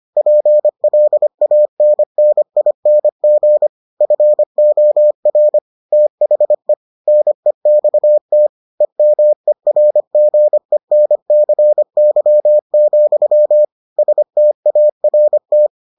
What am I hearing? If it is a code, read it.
Morse code, heard as a keyed tone, about 25 wpm, 600 Hz. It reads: PLANNING FOR THE NEXT EMERGENCY, START